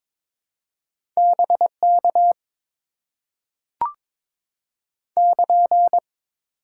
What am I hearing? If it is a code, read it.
Morse code, heard as a keyed tone, about 22 wpm, 700 Hz. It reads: BK <KN>
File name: Top 90 QSO Elements - 8x Character Spacing - 22wpm